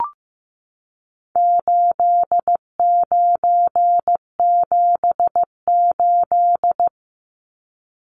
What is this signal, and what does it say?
Morse code, heard as a keyed tone, about 15 wpm, 700 Hz. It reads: 8978
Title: Numbers - 4-Digit 15wpm